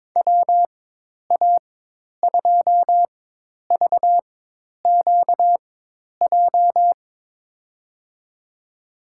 Morse code, heard as a keyed tone, about 22 wpm, spaced as for 10 wpm, 700 Hz. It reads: WA2VQJ
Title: USA Call Signs - 4x Character Spacing - 22wpm